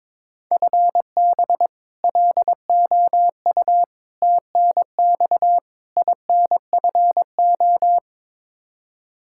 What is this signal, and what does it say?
Morse code, heard as a keyed tone, about 22 wpm, 700 Hz. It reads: FB LOU TNX INFO
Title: K1USN SST QSO Practice 22wpm